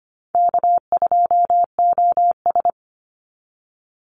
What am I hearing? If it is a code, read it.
Morse code, heard as a keyed tone, about 25 wpm, 700 Hz. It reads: K2OS
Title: Easy Call Signs - 25wpm